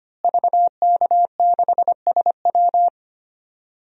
Morse code, heard as a keyed tone, about 25 wpm, 700 Hz. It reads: VK6SW
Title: DXCC-100 Call Signs - 5 Characters - Repeated 2x - 25wpm